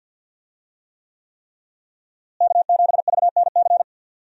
Morse code, heard as a keyed tone, about 50 wpm, 700 Hz. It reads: K7VNC
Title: USA Call Signs - 5 Characters - Repeated 3x - 50wpm